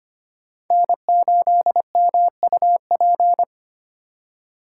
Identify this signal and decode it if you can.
Morse code, heard as a keyed tone, about 25 wpm, 700 Hz. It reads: N8MUP
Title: USA Call Signs - 5 Characters - 4x Character Spacing - 25wpm